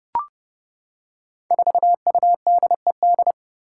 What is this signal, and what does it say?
Morse code, heard as a keyed tone, about 30 wpm, 700 Hz. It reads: 4UDED
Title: Five Challenging Letter-Number Combinations 30wpm